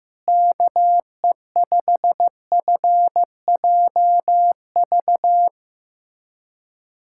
Morse code, heard as a keyed tone, about 16 wpm, 700 Hz. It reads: KE5FJV
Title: USA Callsign DE Callsign - 6 Characters - 15wpm